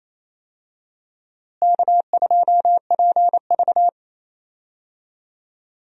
Morse code, heard as a keyed tone, about 28 wpm, 700 Hz. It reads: K2PV
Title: USA Call Signs - 4 Characters - Repeated 3x - 28wpm